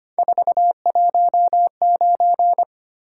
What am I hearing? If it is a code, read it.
Morse code, heard as a keyed tone, about 29 wpm, 700 Hz. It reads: 419
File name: Speed-Racing - Three Letter-Number Combinations - 25wpm